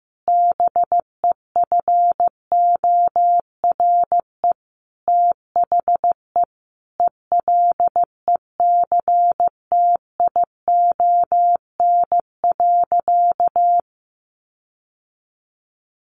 Morse code, heard as a keyed tone, about 15 wpm, 700 Hz. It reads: BEFORE THE ELECTION.
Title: Sentences from Top 1500 Words 15wpm